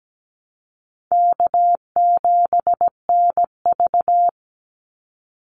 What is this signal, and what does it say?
Morse code, heard as a keyed tone, about 17 wpm, 700 Hz. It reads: K7NV
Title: Call Signs - 17wpm